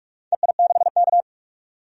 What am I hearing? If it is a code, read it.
Morse code, heard as a keyed tone, about 46 wpm, 700 Hz. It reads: EI6K
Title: DXCC-100 Call Signs - 4 Characters - 45wpm